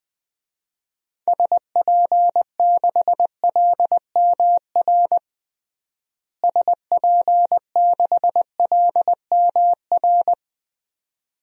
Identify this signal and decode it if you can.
Morse code, heard as a keyed tone, about 21 wpm, 700 Hz. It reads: SP6LMR SP6LMR
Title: DXCC-100 Call Signs - 6 Characters - Repeated 2x - 20wpm